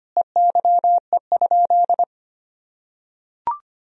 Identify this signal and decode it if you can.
Morse code, heard as a keyed tone, about 25 wpm, 700 Hz. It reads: EYE?
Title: Sentences from Top 500 Words and Abbreviations 25wpm